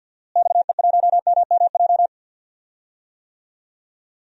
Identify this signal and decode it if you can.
Morse code, heard as a keyed tone, about 47 wpm, 700 Hz. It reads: XE1MMJ
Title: DXCC-100 Call Signs - 6 Characters - Repeated 2x - 50wpm